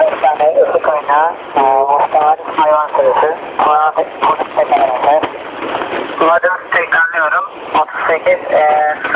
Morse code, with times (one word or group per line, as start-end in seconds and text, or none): none